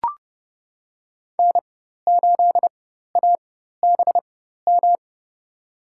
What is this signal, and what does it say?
Morse code, heard as a keyed tone, about 30 wpm, spaced as for 14 wpm, 700 Hz. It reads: N8ABM
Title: USA Call Signs - 5 Characters - 4x Character Spacing - 30wpm